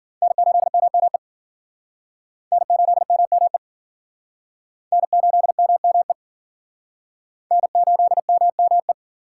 Morse code, heard as a keyed tone, about 50 wpm, 700 Hz. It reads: N8MME N8MME N8MME N8MME
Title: Speed Racing - USA 1x3 Call Signs 40wpm